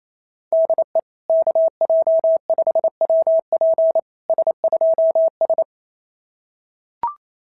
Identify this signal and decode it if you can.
Morse code, heard as a keyed tone, about 28 wpm, 660 Hz. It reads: DE KJ5WP S2S
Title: SOTA QSO Practice 28wpm